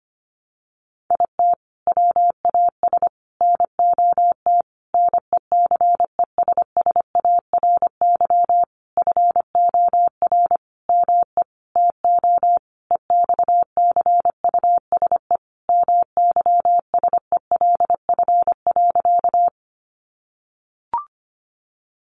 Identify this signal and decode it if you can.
Morse code, heard as a keyed tone, about 25 wpm, 700 Hz. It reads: IT WAS NOT NECESSARY FOR ME TO EXCUSE MYSELF.
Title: Sentences from Top 1300 Words 25wpm